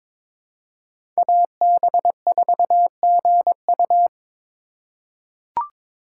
Morse code, heard as a keyed tone, about 22 wpm, 700 Hz. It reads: AB4GU
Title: USA Call Signs - 5 Characters - 4x Character Spacing - 22wpm